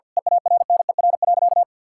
Morse code, heard as a keyed tone, about 49 wpm, 700 Hz. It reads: EARNER.
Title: Pangram Challenge v2 - 50wpm